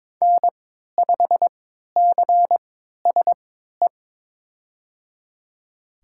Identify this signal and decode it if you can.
Morse code, heard as a keyed tone, about 22 wpm, spaced as for 12 wpm, 700 Hz. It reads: N5CSE
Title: USA Call Signs - 5 Characters - 3x Character Spacing - 22wpm